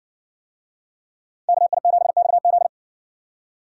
Morse code, heard as a keyed tone, about 55 wpm, 700 Hz. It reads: KI7XZ